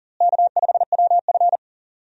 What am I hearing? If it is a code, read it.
Morse code, heard as a keyed tone, about 40 wpm, 700 Hz. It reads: K5WF
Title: USA Call Signs - 4 Characters - Repeated 3x - 40wpm